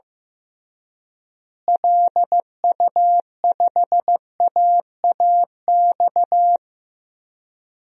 Morse code, heard as a keyed tone, about 15 wpm, 700 Hz. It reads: LU5AAX
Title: DXCC-100 Call Signs - 6 Characters - Repeated 3x - 15wpm